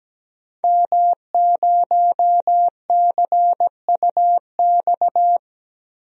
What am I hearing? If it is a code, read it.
Morse code, heard as a keyed tone, about 17 wpm, 700 Hz. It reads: M0CUX
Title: DXCC-100 Call Signs - 5 Characters - Repeated 3x - 17wpm